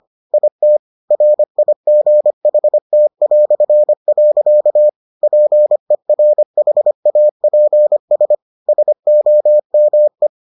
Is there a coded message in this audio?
Yes: Morse code from 0.3 to 10.3 s, heard as a keyed tone, about 25 wpm, 600 Hz. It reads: IT RIGHT". PERHAPS SOME